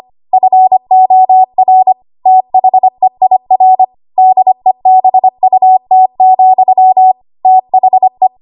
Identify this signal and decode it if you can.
Morse code, heard as a keyed tone, about 25 wpm, 750 Hz. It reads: FOR THEIR DEBUT, THE
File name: Archive / 25 WPM / 25WPM